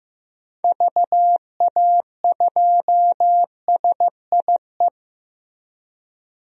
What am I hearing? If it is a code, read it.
Morse code, heard as a keyed tone, about 14 wpm, 700 Hz. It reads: VA2SIE
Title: DXCC-100 Call Signs - 6 Characters - Repeated 3x - 15wpm